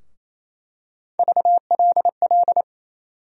Morse code, heard as a keyed tone, about 28 wpm, 700 Hz. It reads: VLL
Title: USA Call Sign Postfixes - 3 Letters 28wpm